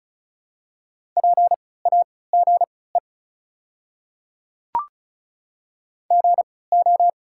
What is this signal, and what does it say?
Morse code, heard as a keyed tone, about 36 wpm, spaced as for 20 wpm, 700 Hz. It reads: PAGE GO
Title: 3x Character Spacing - Top 300 Words - 35wpm